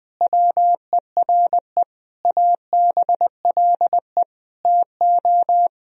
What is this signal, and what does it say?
Morse code, heard as a keyed tone, about 20 wpm, 700 Hz. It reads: WERE ABLE TO
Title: Sentences from Top 1000 Words - Review Part 2 - 20wpm